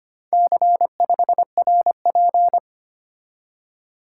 Morse code, heard as a keyed tone, about 25 wpm, 700 Hz. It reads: C5RP
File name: practice / 25wpm / Four Letter-Number Combinations - 25wpm